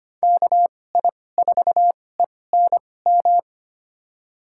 Morse code, heard as a keyed tone, about 25 wpm, spaced as for 18 wpm, 700 Hz. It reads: KI4ENM
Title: USA Call Signs - 6 Characters - 2x Character Spacing - 25wpm